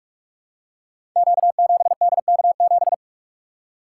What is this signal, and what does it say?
Morse code, heard as a keyed tone, about 44 wpm, 700 Hz. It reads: Q7DK7
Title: Code Groups - 5 Letter:Numbers - 45wpm